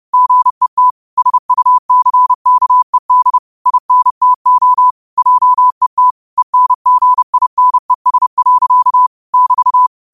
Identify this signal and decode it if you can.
Morse code, heard as a keyed tone, about 30 wpm, 1000 Hz. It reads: GET SUCKED INTO JET ENGINES. <BT>